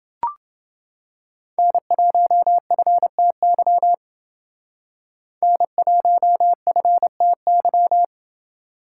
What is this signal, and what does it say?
Morse code, heard as a keyed tone, about 29 wpm, 700 Hz. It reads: N1FTY N1FTY